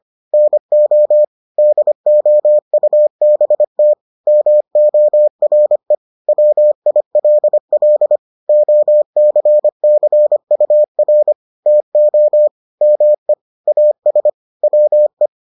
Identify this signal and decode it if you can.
Morse code, heard as a keyed tone, about 25 wpm, 600 Hz. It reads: NO DOUBT MORE WILL OCCUR TO ME AS WE